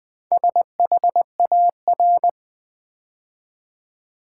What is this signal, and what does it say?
Morse code, heard as a keyed tone, about 20 wpm, 700 Hz. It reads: SHAR
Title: Build A Word - 4 Letters At A Time - 11-Letter Words - 20wpm